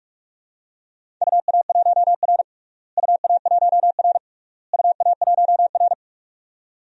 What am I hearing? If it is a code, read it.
Morse code, heard as a keyed tone, about 46 wpm, 700 Hz. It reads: UA1R UA1R UA1R
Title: DXCC-100 Call Signs - 4 Characters - Repeated 3x - 45wpm